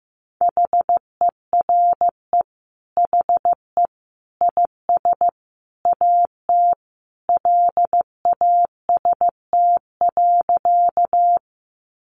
Morse code, heard as a keyed tone, about 15 wpm, 700 Hz. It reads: HERE HE IS AT LAST.
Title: Sentences from Top 300 Words 15wpm